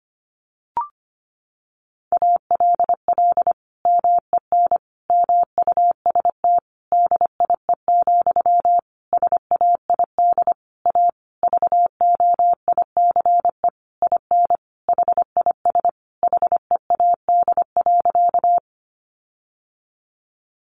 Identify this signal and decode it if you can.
Morse code, heard as a keyed tone, about 25 wpm, 700 Hz. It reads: ALL MEN MUST DIE, SAID A VOICE IN HIS HEAD.
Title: Sentences from Top 1300 Words 25wpm